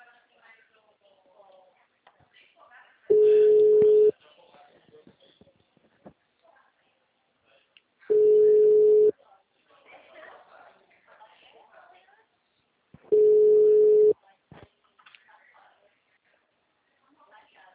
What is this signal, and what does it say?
Ringback tone (Europe, 425 Hz), with speech.